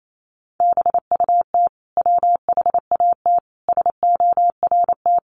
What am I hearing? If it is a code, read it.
Morse code, heard as a keyed tone, about 28 wpm, 700 Hz. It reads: BUT WHAT SORT
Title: Sentences from Top 1300 Words 28wpm